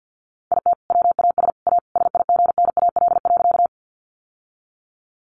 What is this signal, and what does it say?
Morse code, heard as a keyed tone, about 50 wpm, 700 Hz. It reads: IT WAS A SIGNAL.
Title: Sentences from Top 1400 Words 50wpm